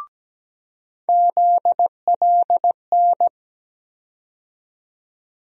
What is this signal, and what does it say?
Morse code, heard as a keyed tone, about 17 wpm, 700 Hz. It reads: ZLN
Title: USA Call Sign Postfixes - 3 Letters 17wpm